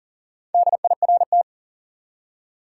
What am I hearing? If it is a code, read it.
Morse code, heard as a keyed tone, about 38 wpm, 700 Hz. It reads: DIRT
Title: Four Letter Words - 40wpm